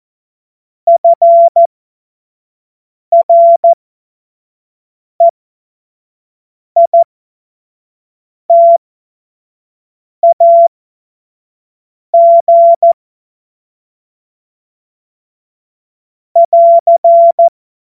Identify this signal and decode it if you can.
Morse code, heard as a keyed tone, about 14 wpm, spaced as for 5 wpm, 680 Hz. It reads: FREITAG <AR>
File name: Morsekurs / Witze / Witz 20 Mai 2025 25 F70